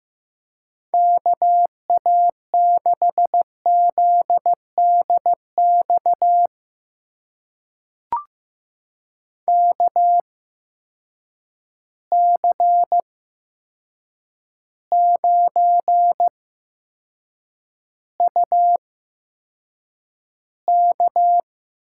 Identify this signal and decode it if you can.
Morse code, heard as a keyed tone, about 15 wpm, spaced as for 7 wpm, 700 Hz. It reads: KA6ZDX KC9UK
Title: USA Call Signs - 6 Characters - 8x Character Spacing - 15wpm